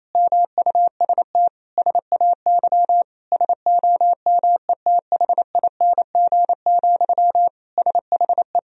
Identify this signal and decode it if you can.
Morse code, heard as a keyed tone, about 28 wpm, 700 Hz. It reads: MUST SAY SOMETHING, SHE
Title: Sentences from Top 1000 Words - Review Part 3 - 28wpm